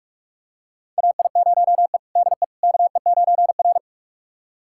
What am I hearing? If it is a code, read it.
Morse code, heard as a keyed tone, about 46 wpm, 700 Hz. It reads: AI0E DE KE9R
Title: USA Callsign DE Callsign - 4 Characters - 45wpm